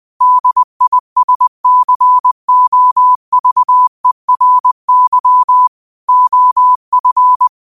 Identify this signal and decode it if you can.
Morse code, heard as a keyed tone, about 20 wpm, 1000 Hz. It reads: DISCOVERY OF
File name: qotdfiles / 20wpm